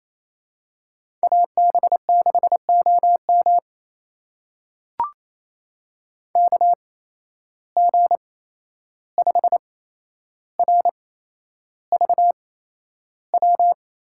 Morse code, heard as a keyed tone, about 28 wpm, spaced as for 11 wpm, 700 Hz. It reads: AB6OM KG5RVW
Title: USA Call Signs - 8x Character Spacing - 28wpm